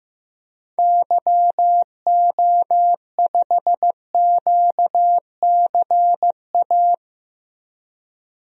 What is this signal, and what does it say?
Morse code, heard as a keyed tone, about 15 wpm, 700 Hz. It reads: YO5QCA
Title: DXCC-100 Call Signs - 6 Characters - 15wpm